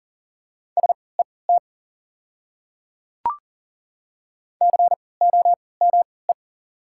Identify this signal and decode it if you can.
Morse code, heard as a keyed tone, about 40 wpm, spaced as for 23 wpm, 700 Hz. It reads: SET COME
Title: 3x Character Spacing - Top 300 Words - 40wpm